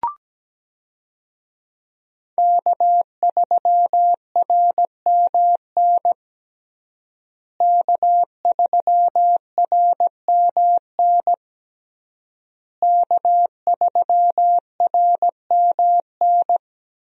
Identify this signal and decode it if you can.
Morse code, heard as a keyed tone, about 17 wpm, 700 Hz. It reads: K3RMN K3RMN K3RMN
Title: USA Call Signs - 5 Characters - Repeated 3x - 17wpm